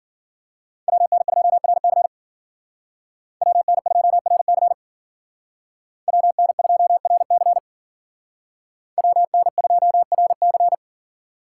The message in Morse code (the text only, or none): WN2RC WN2RC WN2RC WN2RC